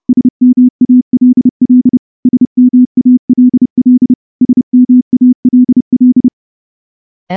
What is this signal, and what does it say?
Morse code, heard as a keyed tone, about 28 wpm, 250 Hz. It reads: SMALL SMALL SMALL
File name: morse-training-30WPM